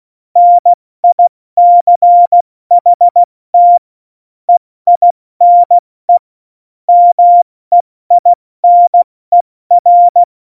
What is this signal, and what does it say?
Morse code, heard as a keyed tone, about 16 wpm, spaced as for 14 wpm, 700 Hz. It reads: NICHT EINE MEINER
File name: Morsekurs / Witze / Witz 18. Juni 2019, 70 F80